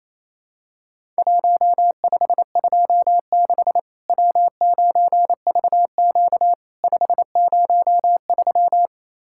Morse code, heard as a keyed tone, about 28 wpm, 700 Hz. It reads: 1526 W9VQ 503
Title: Worked All Europe DX Contest - QTC Exchange - 28wpm